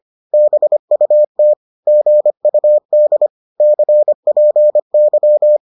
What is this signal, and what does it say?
Morse code, heard as a keyed tone, about 25 wpm, 600 Hz. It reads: BUT GUD CPY